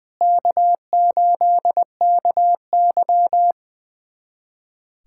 Morse code, heard as a keyed tone, about 21 wpm, 700 Hz. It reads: K8KY
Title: USA Callsign DE Callsign 20wpm